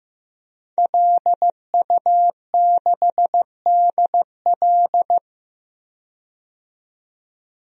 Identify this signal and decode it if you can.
Morse code, heard as a keyed tone, about 15 wpm, 700 Hz. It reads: LU6DL
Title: DXCC-100 Call Signs - 5 Characters - 15wpm